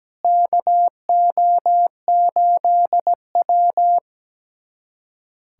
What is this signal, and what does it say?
Morse code, heard as a keyed tone, about 17 wpm, 700 Hz. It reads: KO8W
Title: USA Callsign DE Callsign - 4 Characters - 17wpm